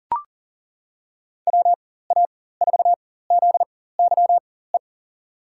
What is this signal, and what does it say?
Morse code, heard as a keyed tone, about 39 wpm, spaced as for 18 wpm, 700 Hz. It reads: WA4ZYE
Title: USA Call Signs - 4x Character Spacing - 40wpm